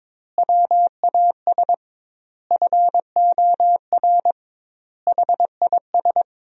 Morse code, heard as a keyed tone, about 22 wpm, 700 Hz. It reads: WAS FOR HIS
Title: 2x Word-Spacing - Sentences from Top 500 Words - 22wpm